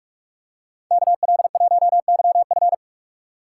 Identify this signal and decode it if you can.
Morse code, heard as a keyed tone, about 43 wpm, 700 Hz. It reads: KL1YF